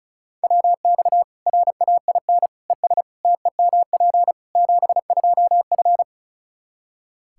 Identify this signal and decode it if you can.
Morse code, heard as a keyed tone, about 36 wpm, 700 Hz. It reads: WX RAIN ES TEMP 72F
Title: K1USN SST QSO Practice 35wpm